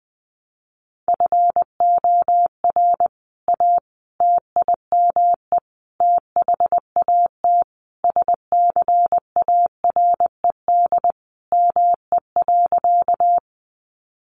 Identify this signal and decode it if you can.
Morse code, heard as a keyed tone, about 19 wpm, 700 Hz. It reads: FOR A TIME THAT SCARED ME.
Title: Sentences from Top 700 Words 20wpm